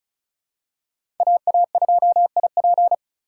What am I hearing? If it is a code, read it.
Morse code, heard as a keyed tone, about 34 wpm, 700 Hz. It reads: AA2IP
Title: USA Call Signs - 5 Characters - 3x Character Spacing - 35wpm